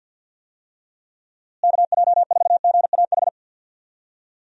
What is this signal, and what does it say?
Morse code, heard as a keyed tone, about 50 wpm, 700 Hz. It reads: KJ4GAH
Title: USA Call Signs - 6 Characters - Repeated 3x - 50wpm